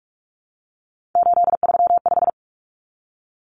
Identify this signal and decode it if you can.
Morse code, heard as a keyed tone, about 44 wpm, 700 Hz. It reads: 835
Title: Numbers - 3-Digit 45wpm